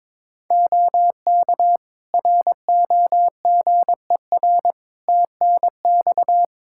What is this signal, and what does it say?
Morse code, heard as a keyed tone, about 22 wpm, 700 Hz. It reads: OK ROGER TNX
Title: Practice QSO - Exchange 4 - Ending - 22wpm